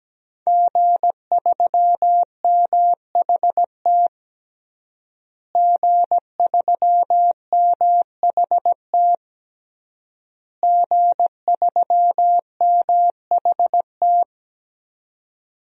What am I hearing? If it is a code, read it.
Morse code, heard as a keyed tone, about 17 wpm, 700 Hz. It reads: G3MHT G3MHT G3MHT
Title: DXCC-100 Call Signs - 5 Characters - Repeated 3x - 17wpm